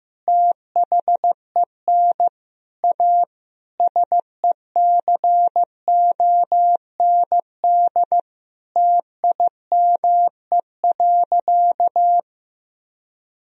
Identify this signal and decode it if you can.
Morse code, heard as a keyed tone, about 15 wpm, 700 Hz. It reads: THEN A SECOND TIME.
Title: Sentences from Top 1000 Words - Review Part 1 - 15wpm